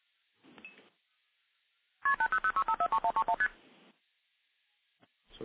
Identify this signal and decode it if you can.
Multi-frequency trunk signalling, heard as fast pairs of tones, tones of about 80 ms. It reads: KP8006543131ST